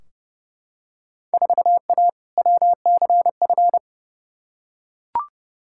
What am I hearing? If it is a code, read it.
Morse code, heard as a keyed tone, about 30 wpm, 700 Hz. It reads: 4A WCF
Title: Field Day Exchanges - 30wpm